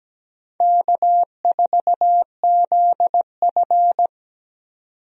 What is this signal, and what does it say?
Morse code, heard as a keyed tone, about 17 wpm, 700 Hz. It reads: K4ZF
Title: USA Call Signs - 4 Characters - 17wpm